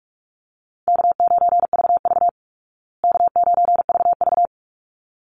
Morse code, heard as a keyed tone, about 44 wpm, 700 Hz. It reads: K9VV K9VV
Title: Call Signs Repeated 3x - 45wpm